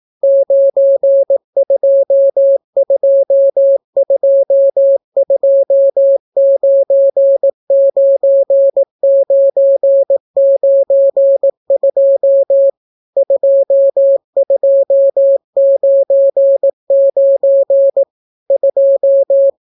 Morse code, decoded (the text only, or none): 9222299992 2299 2